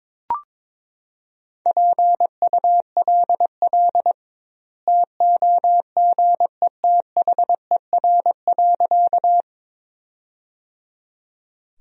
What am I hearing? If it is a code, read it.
Morse code, heard as a keyed tone, about 22 wpm, 700 Hz. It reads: PULL TOGETHER.
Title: 2x Word-Spacing - Sentences from Top 400 Words - 22wpm